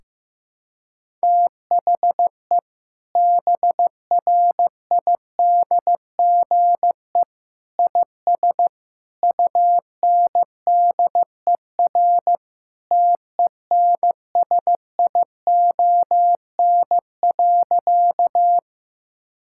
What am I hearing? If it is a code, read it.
Morse code, heard as a keyed tone, about 15 wpm, 700 Hz. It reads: THE BRIDGE IS UNDER TENSION.
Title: Sentences from Top 1600 Words 15wpm